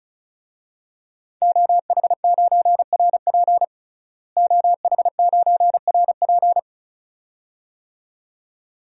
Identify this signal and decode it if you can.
Morse code, heard as a keyed tone, about 34 wpm, 700 Hz. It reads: OH9RP OH9RP
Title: DXCC-100 Call Signs - 5 Characters - Repeated 2x - 35wpm